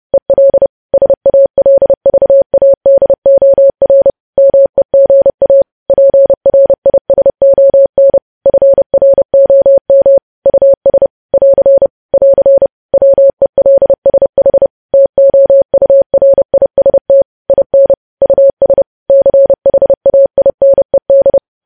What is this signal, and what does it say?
Morse code, heard as a keyed tone, about 30 wpm, 550 Hz. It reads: EL SALVADOR MEGA PRISON FROM US <AR> <AR> WELSH TOURIST IN US CHAINED
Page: news headlines in morse code at 30 wpm